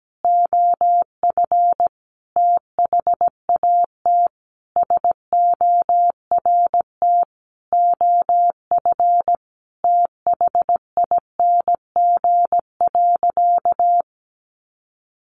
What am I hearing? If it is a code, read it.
Morse code, heard as a keyed tone, about 17 wpm, 700 Hz. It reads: OF THAT SORT OF THING.
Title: Sentences from Top 1300 Words 17wpm